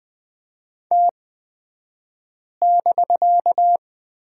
Morse code, heard as a keyed tone, about 20 wpm, spaced as for 5 wpm, 700 Hz. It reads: T<BK>
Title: 249 Lesson 40d - BK - Two Characters 20wpm